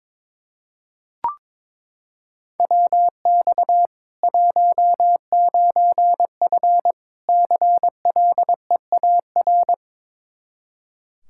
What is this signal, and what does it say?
Morse code, heard as a keyed tone, about 22 wpm, 700 Hz. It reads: WX 19F CLEAR
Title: Practice QSO - Exchange 2 - Info - 22wpm